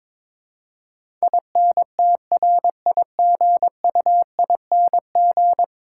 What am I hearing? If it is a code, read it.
Morse code, heard as a keyed tone, about 22 wpm, 700 Hz. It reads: INTRIGUING
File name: practice / 22wpm / 10 Letter Words - First 3 Letters Given - 22wpm